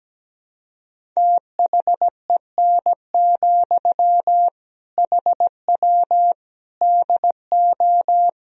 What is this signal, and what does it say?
Morse code, heard as a keyed tone, about 17 wpm, 700 Hz. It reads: THEN, HW DO
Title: Sentences from Top 100 Words and Abbreviations 17wpm